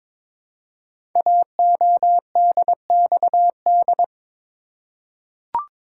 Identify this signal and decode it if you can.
Morse code, heard as a keyed tone, about 22 wpm, 700 Hz. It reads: AODXD
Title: Code Groups - 5 Letters - 22wpm